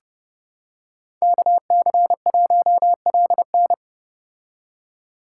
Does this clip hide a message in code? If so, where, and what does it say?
Yes: Morse code from 1.2 to 3.7 s, heard as a keyed tone, about 30 wpm, 700 Hz. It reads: KC1LN